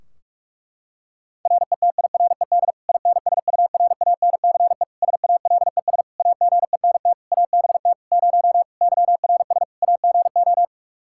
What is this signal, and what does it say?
Morse code, heard as a keyed tone, about 46 wpm, 700 Hz. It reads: RETIRED INSURANCE SALES AGENT ABT 0 YRS AGO
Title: Practice QSO - Exchange 3 - More Info - 45wpm